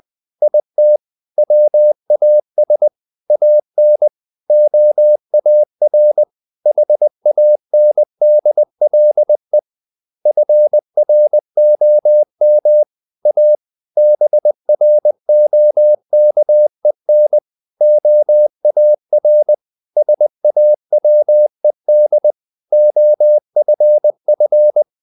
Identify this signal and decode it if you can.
Morse code, heard as a keyed tone, about 20 wpm, 600 Hz. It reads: IT WAS AN OAR HANDLE FROM A BROKEN OAR SAWED OFF
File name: the old man and the sea part 2 20wpm